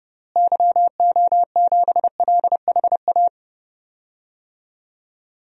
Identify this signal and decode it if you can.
Morse code, heard as a keyed tone, about 30 wpm, 700 Hz. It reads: YO7LHA
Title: DXCC-100 Call Signs - 6 Characters - Repeated 3x - 30wpm